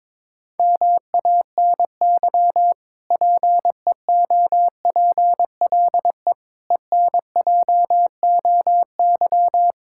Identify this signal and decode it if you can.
Morse code, heard as a keyed tone, about 22 wpm, 700 Hz. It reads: MANY PEOPLE ENJOY
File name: Difficult Sentences with Hints - 22wpm